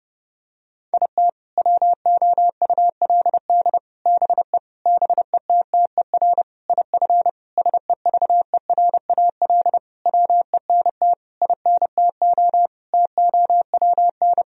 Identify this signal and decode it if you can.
Morse code, heard as a keyed tone, about 30 wpm, 700 Hz. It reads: IT WOULD BE BETTER IF SEVERAL WENT INTO TOWN
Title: Sentences from Top 1000 Words - Review Part 2 - 30wpm